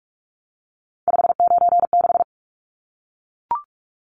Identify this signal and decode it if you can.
Morse code, heard as a keyed tone, about 44 wpm, 700 Hz. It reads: 596
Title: Numbers - 3-Digit 45wpm